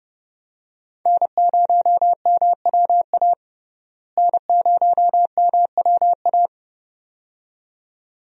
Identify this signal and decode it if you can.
Morse code, heard as a keyed tone, about 30 wpm, 700 Hz. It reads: N0MWA N0MWA
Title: USA Call Signs - 5 Characters - Repeated 2x - 30wpm